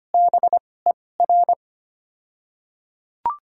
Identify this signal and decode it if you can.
Morse code, heard as a keyed tone, about 25 wpm, spaced as for 18 wpm, 690 Hz. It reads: BER
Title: 2x Character Spacing - Top 100 Words - 25wpm